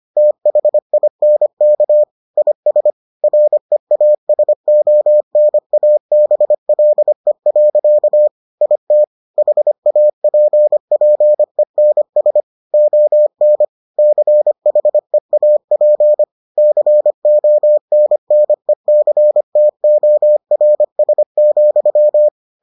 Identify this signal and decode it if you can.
Morse code, heard as a keyed tone, about 25 wpm, 600 Hz. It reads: THINK IS REASONABLE. IT HAPPENS ON CHEAP CONNECTORS,